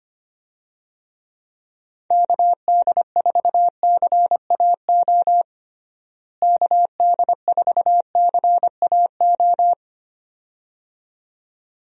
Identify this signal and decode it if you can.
Morse code, heard as a keyed tone, about 25 wpm, 700 Hz. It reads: KD4CAO KD4CAO
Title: USA Call Signs - 6 Characters - Repeated 2x - 25wpm